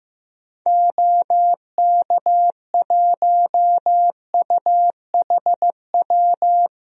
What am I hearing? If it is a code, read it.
Morse code, heard as a keyed tone, about 16 wpm, 700 Hz. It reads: OK1UHW
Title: DXCC-100 Call Signs - 6 Characters - Repeated 3x - 15wpm